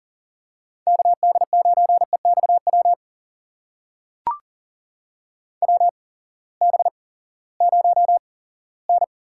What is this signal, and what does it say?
Morse code, heard as a keyed tone, about 40 wpm, spaced as for 20 wpm, 700 Hz. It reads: KD9EXW WB0N
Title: USA Call Signs - 6 Characters - 8x Character Spacing - 40wpm